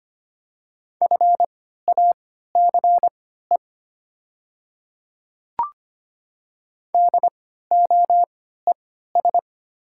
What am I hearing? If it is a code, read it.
Morse code, heard as a keyed tone, about 25 wpm, spaced as for 14 wpm, 700 Hz. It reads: FACE DOES
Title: 3x Character Spacing - Top 300 Words - 25wpm